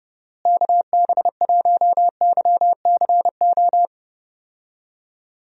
Morse code, heard as a keyed tone, about 30 wpm, 700 Hz. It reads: KB1YCO
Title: USA Call Signs - 6 Characters - Repeated 2x - 30wpm